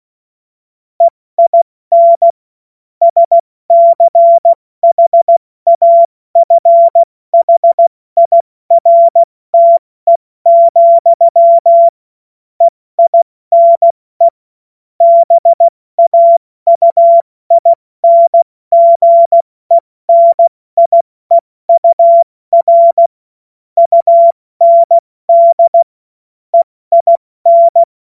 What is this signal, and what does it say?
Morse code, heard as a keyed tone, about 16 wpm, spaced as for 14 wpm, 680 Hz. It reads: EIN SCHAFHIRTE, EINE BAUINGENIEUR UND EIN
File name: Morsekurs / Witze / Witz 26 Dezember 2023 70 F80